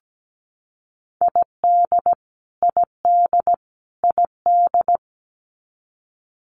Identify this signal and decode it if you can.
Morse code, heard as a keyed tone, about 17 wpm, 700 Hz. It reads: ID ID ID
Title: US States Repeated 3x 17wpm